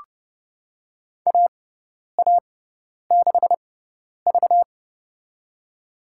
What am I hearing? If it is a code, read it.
Morse code, heard as a keyed tone, about 29 wpm, spaced as for 10 wpm, 700 Hz. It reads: AA6V
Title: USA Call Signs - 4 Characters - 6x Character Spacing - 30wpm